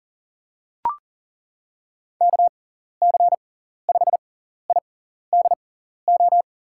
Morse code, heard as a keyed tone, about 40 wpm, spaced as for 14 wpm, 700 Hz. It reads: KC5IDO
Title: USA Call Signs - 6 Characters - 6x Character Spacing - 40wpm